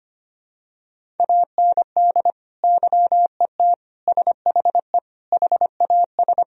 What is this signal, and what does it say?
Morse code, heard as a keyed tone, about 25 wpm, 700 Hz. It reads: AND YET SHE HAS
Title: Sentences from Top 1000 Words - Review Part 1 - 25wpm